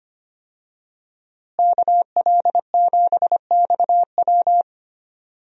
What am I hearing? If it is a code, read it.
Morse code, heard as a keyed tone, about 25 wpm, 700 Hz. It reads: KL7XW
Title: DXCC-100 Call Signs - 5 Characters - Repeated 2x - 25wpm